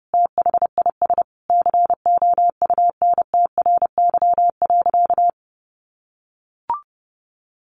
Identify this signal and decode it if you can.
Morse code, heard as a keyed tone, about 30 wpm, 700 Hz. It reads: THIS COUNTRY.
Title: Sentences from Top 1400 Words 30wpm